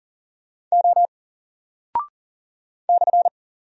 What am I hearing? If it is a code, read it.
Morse code, heard as a keyed tone, about 39 wpm, 700 Hz. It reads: O /
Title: Single Character (Letter:Number:Punctuation:Prosign) 40wpm